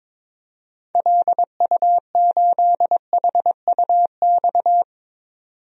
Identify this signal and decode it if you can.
Morse code, heard as a keyed tone, about 22 wpm, 700 Hz. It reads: LU8HUX
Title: DXCC-100 Call Signs - 6 Characters - 22wpm